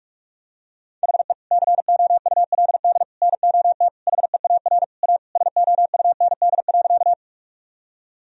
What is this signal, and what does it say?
Morse code, heard as a keyed tone, about 41 wpm, 700 Hz. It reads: HE COULD NOT HEAR A SOUND.
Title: Sentences from Top 1000 Words - Review Part 1 - 45wpm